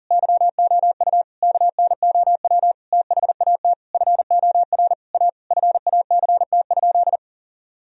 Morse code, heard as a keyed tone, about 40 wpm, 700 Hz. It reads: YOU KNOW THAT FOR A FACT?
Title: Sentences from Top 1000 Words - Review Part 3 - 40wpm